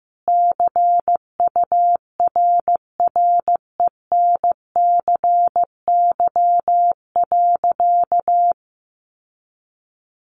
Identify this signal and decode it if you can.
Morse code, heard as a keyed tone, about 15 wpm, 700 Hz. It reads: CURRENCY.
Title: Sentences from Top 1500 Words 15wpm